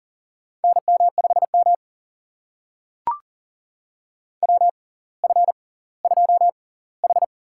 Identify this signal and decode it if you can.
Morse code, heard as a keyed tone, about 40 wpm, spaced as for 20 wpm, 700 Hz. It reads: NM5M WF2H
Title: USA Call Signs - 4 Characters - 6x Character Spacing - 40wpm